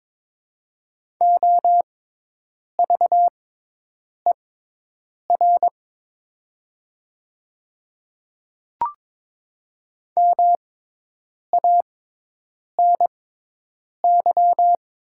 Morse code, heard as a keyed tone, about 23 wpm, spaced as for 8 wpm, 700 Hz. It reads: OVER MANY
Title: 6x Character Spacing - Top 100 Words - 22wpm